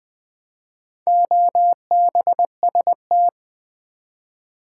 Morse code, heard as a keyed tone, about 20 wpm, 700 Hz. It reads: OBST
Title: Build A Word - 4 Letters At A Time - 8-Letter Words - 20wpm